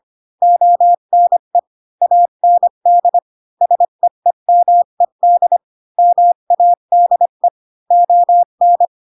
Morse code, heard as a keyed tone, about 25 wpm, 700 Hz. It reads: ONE AND SEEMED MADE ON
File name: Pinocchio / Pinocchio ch3 20 25 ik1hge